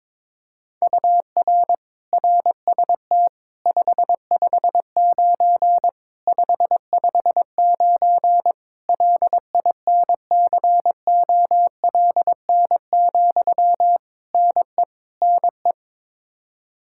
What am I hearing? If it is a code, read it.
Morse code, heard as a keyed tone, about 22 wpm, 700 Hz. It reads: UR RST 559 559 LINCOLN, NE NE
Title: Practice QSO - Exchange 1 - Beginning - 22wpm